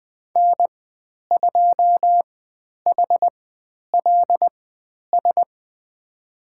Morse code, heard as a keyed tone, about 20 wpm, spaced as for 10 wpm, 700 Hz. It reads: N2HLS
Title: 095 Lesson 16g - Letter c - Call Signs 20wpm